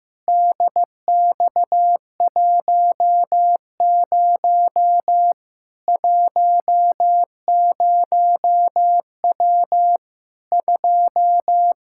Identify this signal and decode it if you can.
Morse code, heard as a keyed tone, about 15 wpm, 700 Hz. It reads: DX10 10W 2
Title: K1USN SST QSO Practice 15wpm